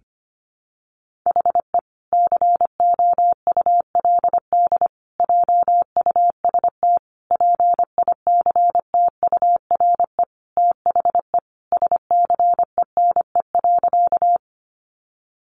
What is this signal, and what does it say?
Morse code, heard as a keyed tone, about 25 wpm, 700 Hz. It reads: HE COULD JUST PICTURE THE SCENE.
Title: Sentences from Top 1100 Words 25wpm